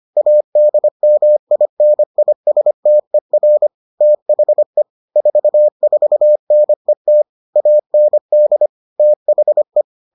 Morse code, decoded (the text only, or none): ADMINISTER THE 44NET AND THE